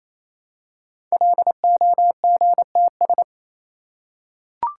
Morse code, heard as a keyed tone, about 29 wpm, 700 Hz. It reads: LOGTS
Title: Code Groups - 5 Letter:Numbers - 28wpm